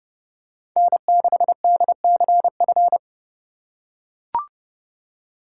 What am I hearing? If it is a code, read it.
Morse code, heard as a keyed tone, about 32 wpm, 700 Hz. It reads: N6DCF
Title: USA Call Signs - 5 Characters - 2x Character Spacing - 30wpm